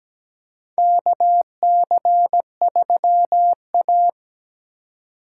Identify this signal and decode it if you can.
Morse code, heard as a keyed tone, about 17 wpm, 700 Hz. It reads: KC3A
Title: USA Call Signs - 4 Characters - 3x Character Spacing - 17wpm